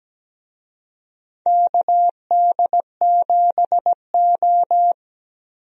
Morse code, heard as a keyed tone, about 17 wpm, 700 Hz. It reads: KD7O
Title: USA Call Signs - 4 Characters - Repeated 3x - 17wpm